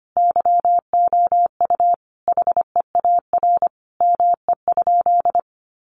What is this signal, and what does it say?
Morse code, heard as a keyed tone, about 25 wpm, 700 Hz. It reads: YOU HEAR ME?
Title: Sentences from Top 300 Words 25wpm